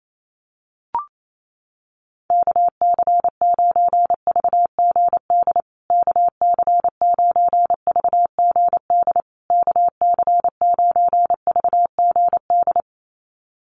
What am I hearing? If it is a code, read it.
Morse code, heard as a keyed tone, about 28 wpm, 700 Hz. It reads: KC9VGD KC9VGD KC9VGD
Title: USA Call Signs Repeated 3x 28wpm